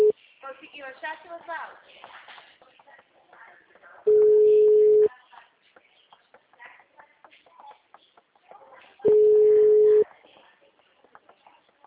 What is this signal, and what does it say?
Ringback tone (Europe, 425 Hz), with speech.